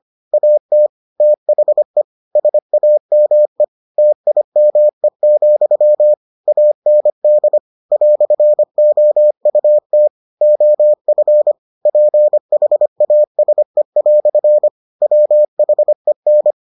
Morse code, heard as a keyed tone, about 25 wpm, 600 Hz. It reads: AT THE SAME TIME, AND "OUT OF PHASE" WHEN